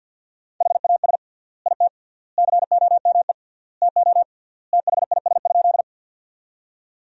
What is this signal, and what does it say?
Morse code, heard as a keyed tone, about 50 wpm, 700 Hz. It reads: HAS IT COME TO THIS?
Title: 3x Word-Spacing - Sentences from Top 100 Words - 50wpm